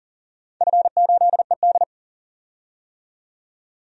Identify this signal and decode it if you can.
Morse code, heard as a keyed tone, about 40 wpm, 700 Hz. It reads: F8ED
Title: DXCC-100 Call Signs - 4 Characters - 40wpm